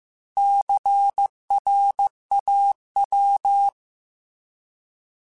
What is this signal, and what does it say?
Morse code, heard as a keyed tone, about 15 wpm, 800 Hz. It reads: CRAW